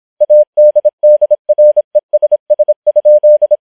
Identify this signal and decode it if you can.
Morse code, heard as a keyed tone, about 26 wpm, 600 Hz. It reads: ADDRESS?